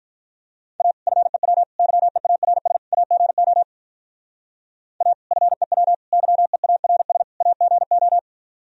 Morse code, heard as a keyed tone, about 43 wpm, 700 Hz. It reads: A FEW YEARS AGO A FEW YEARS AGO
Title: Speed-Racing - Sets of 4 Words 35wpm